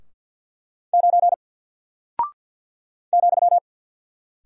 Morse code, heard as a keyed tone, about 48 wpm, 700 Hz. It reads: 9 ,
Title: Single Character (Letter:Number:Punctuation:Prosign) Alt 50wpm